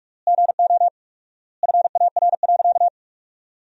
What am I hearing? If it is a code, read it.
Morse code, heard as a keyed tone, about 41 wpm, 700 Hz. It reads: GO FAR.
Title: 4x Word-Spacing - Sentences from Top 400 Words - 45wpm